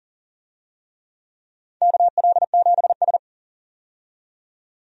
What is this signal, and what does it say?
Morse code, heard as a keyed tone, about 40 wpm, 700 Hz. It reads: KL7S
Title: DXCC-100 Call Signs - 4 Characters - 40wpm